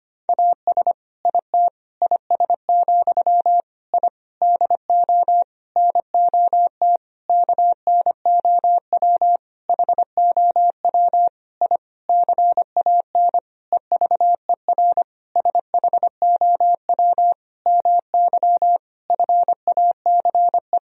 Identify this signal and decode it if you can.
Morse code, heard as a keyed tone, about 25 wpm, 700 Hz. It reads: AS IT IS, I DO NOT KNOW HOW I CAN EVER SHOW MY FACE
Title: Sentences from Top 1000 Words - Review Part 2 - 25wpm